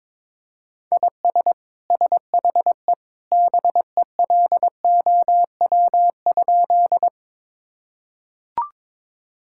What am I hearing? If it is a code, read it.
Morse code, heard as a keyed tone, about 21 wpm, 700 Hz. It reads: IS SHE BELOW?
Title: Sentences from Top 1000 Words - Review Part 5 - 22wpm